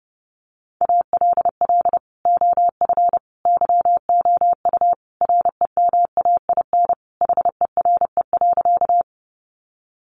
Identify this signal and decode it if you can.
Morse code, heard as a keyed tone, about 30 wpm, 700 Hz. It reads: ALL OF YOU REMAIN HERE.
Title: Sentences from Top 1200 Words 30wpm